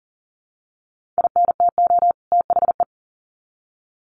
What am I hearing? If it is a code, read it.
Morse code, heard as a keyed tone, about 40 wpm, 700 Hz. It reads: INTO THE
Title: Sets of 2 Words 40wpm